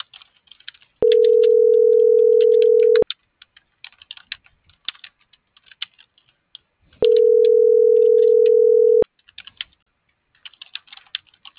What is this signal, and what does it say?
Ringback tone (North America), 440 + 480 Hz.